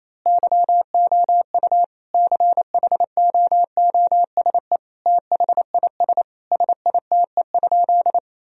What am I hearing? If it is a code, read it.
Morse code, heard as a keyed tone, about 27 wpm, 700 Hz. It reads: YOU CHOOSE THIS SITE?
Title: Sentences from Top 1000 Words - Review Part 4 - 28wpm